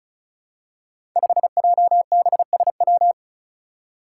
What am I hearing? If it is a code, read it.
Morse code, heard as a keyed tone, about 36 wpm, 700 Hz. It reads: 5JBSW
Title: Five Challenging Letter-Number Combinations 35wpm